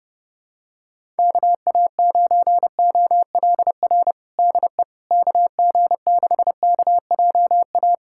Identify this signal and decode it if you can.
Morse code, heard as a keyed tone, about 30 wpm, 700 Hz. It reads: KA9OLR DE KG6KJA
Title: USA Callsign DE Callsign - 6 Characters - 30wpm